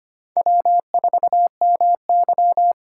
Morse code, heard as a keyed tone, about 32 wpm, 700 Hz. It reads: W4MY